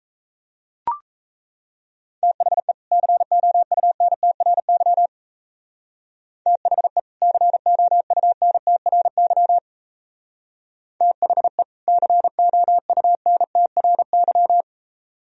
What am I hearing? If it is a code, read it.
Morse code, heard as a keyed tone, about 32 wpm, 700 Hz. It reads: THE COUNTRY THE COUNTRY THE COUNTRY
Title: Speed-Racing - Sets of 2 Words 28wpm